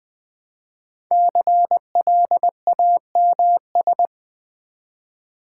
Morse code, heard as a keyed tone, about 20 wpm, 700 Hz. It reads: CLAMS
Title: Difficult Words with Hints - 20wpm